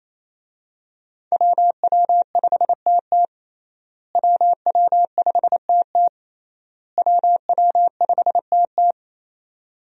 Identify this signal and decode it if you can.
Morse code, heard as a keyed tone, about 28 wpm, 700 Hz. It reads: WW5TT WW5TT WW5TT
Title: USA Call Signs - 5 Characters - Repeated 3x - 28wpm